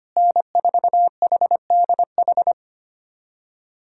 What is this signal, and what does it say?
Morse code, heard as a keyed tone, about 25 wpm, 700 Hz. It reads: N4HDH
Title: USA Callsign DE Callsign - 5 Characters - 25wpm